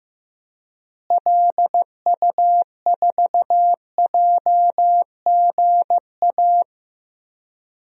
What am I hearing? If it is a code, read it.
Morse code, heard as a keyed tone, about 15 wpm, 700 Hz. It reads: LU4JGA
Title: DXCC-100 Call Signs - 6 Characters - Repeated 3x - 15wpm